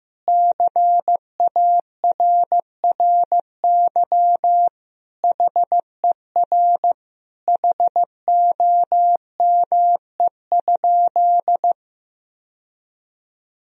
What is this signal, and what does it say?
Morse code, heard as a keyed tone, about 15 wpm, 700 Hz. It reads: CARRY HER HOME?
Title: Sentences from Top 1000 Words - Review Part 4 - 15wpm